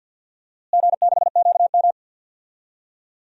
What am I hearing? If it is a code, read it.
Morse code, heard as a keyed tone, about 53 wpm, 700 Hz. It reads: G6QM
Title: DXCC-100 Call Signs - 4 Characters - Repeated 2x - 50wpm